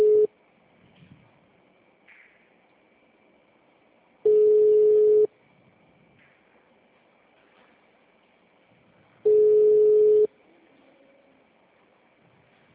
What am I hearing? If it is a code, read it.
Ringback tone (Europe, 425 Hz).
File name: out- -2006- - - .378